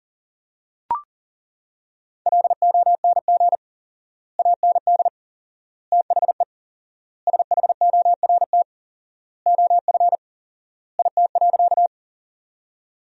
Morse code, heard as a keyed tone, about 40 wpm, 700 Hz. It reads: LONG AND THE SHORT OF IT.
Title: 4x Word-Spacing - Sentences from Top 400 Words - 40wpm